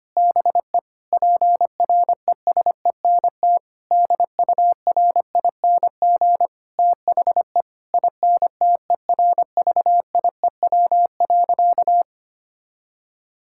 Morse code, heard as a keyed tone, about 25 wpm, 700 Hz. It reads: BE PRESENT DURING THE INTERVIEW.
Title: Sentences from Top 1600 Words 25wpm